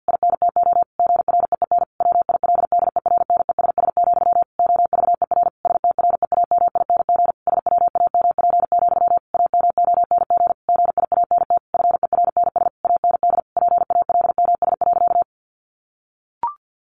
Morse code, heard as a keyed tone, about 50 wpm, 700 Hz. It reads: INTO GREEN WILDERNESS, OVER STREAMING SWAMP, AMONG GIANT FERNS AND PALMS.
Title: A Sound Of Thunder 50wpm